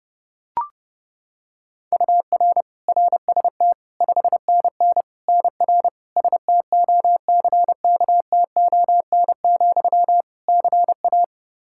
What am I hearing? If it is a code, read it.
Morse code, heard as a keyed tone, about 30 wpm, 700 Hz. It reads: UR RST 5NN NR STOCKTON, CA
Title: Practice QSO - Exchange 1 - Beginning - 30wpm